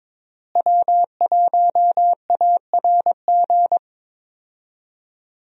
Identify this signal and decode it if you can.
Morse code, heard as a keyed tone, about 22 wpm, 700 Hz. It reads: W1ARG
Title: USA Call Signs - 5 Characters - Repeated 2x - 22wpm